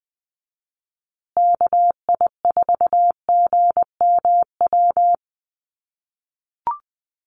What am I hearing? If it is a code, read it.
Morse code, heard as a keyed tone, about 20 wpm, 700 Hz. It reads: KI4GMW
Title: USA Call Signs Repeated 3x 20wpm